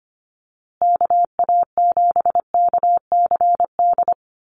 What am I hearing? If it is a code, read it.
Morse code, heard as a keyed tone, about 25 wpm, 700 Hz. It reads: KA7KCD
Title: USA Call Signs Repeated 3x 25wpm